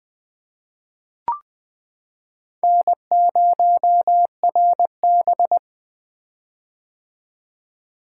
Morse code, heard as a keyed tone, about 20 wpm, 700 Hz. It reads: N0RB
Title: Four Letter-Number Combinations - 20wpm